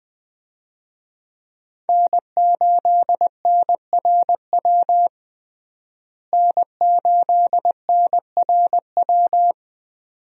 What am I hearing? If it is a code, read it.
Morse code, heard as a keyed tone, about 21 wpm, 700 Hz. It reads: N8NRW N8NRW
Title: USA Call Signs - 5 Characters - Repeated 2x - 20wpm